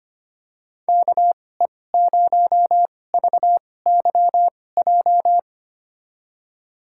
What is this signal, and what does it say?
Morse code, heard as a keyed tone, about 25 wpm, spaced as for 18 wpm, 700 Hz. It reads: KE0VYJ
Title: USA Call Signs - 6 Characters - 2x Character Spacing - 25wpm